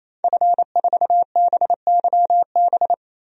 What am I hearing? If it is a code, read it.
Morse code, heard as a keyed tone, about 28 wpm, 700 Hz. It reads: F4BYB
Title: DXCC-100 Call Signs - 5 Characters - 28wpm